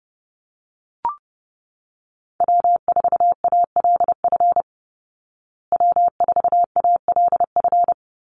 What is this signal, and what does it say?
Morse code, heard as a keyed tone, about 30 wpm, 700 Hz. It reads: W4ALF W4ALF
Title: Call Signs Repeated 2x - 30wpm